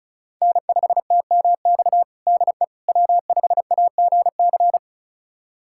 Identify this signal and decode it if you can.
Morse code, heard as a keyed tone, about 36 wpm, 700 Hz. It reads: N5TMX DE W5AGC
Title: USA Callsign DE Callsign - 5 Characters - 35wpm